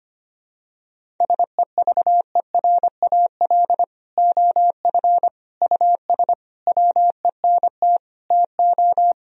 Morse code, heard as a keyed tone, about 25 wpm, 700 Hz. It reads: SEVERAL OF US WENT TO
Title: Sentences from Top 1000 Words - Review Part 2 - 25wpm